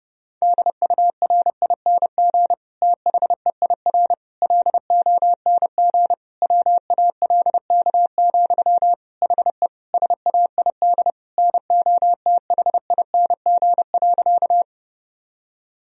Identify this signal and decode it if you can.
Morse code, heard as a keyed tone, about 32 wpm, 700 Hz. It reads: DURING THEIR LONG WALK, HE SAID NOTHING.
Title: Sentences from Top 1000 Words - Review Part 2 - 30wpm